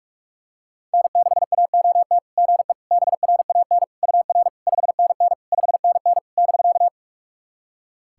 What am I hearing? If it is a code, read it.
Morse code, heard as a keyed tone, about 46 wpm, 690 Hz. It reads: N6AOT GE BRAN UR 5NN 5NN <BK>
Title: SOTA QSO Practice 45wpm